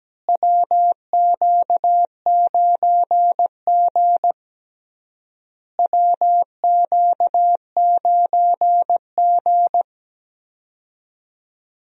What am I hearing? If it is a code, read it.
Morse code, heard as a keyed tone, about 17 wpm, 700 Hz. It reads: WQ9G WQ9G
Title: USA Call Signs - 4 Characters - Repeated 3x - 17wpm